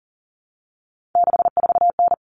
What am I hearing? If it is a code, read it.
Morse code, heard as a keyed tone, about 40 wpm, 700 Hz. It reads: 64N
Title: Common Cut Numbers - 3-Digit 40wpm